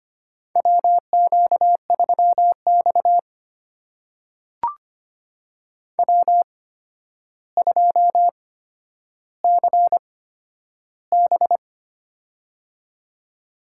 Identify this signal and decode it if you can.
Morse code, heard as a keyed tone, about 25 wpm, spaced as for 11 wpm, 700 Hz. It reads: WQ3X W2CB
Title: USA Call Signs - 4 Characters - 8x Character Spacing - 25wpm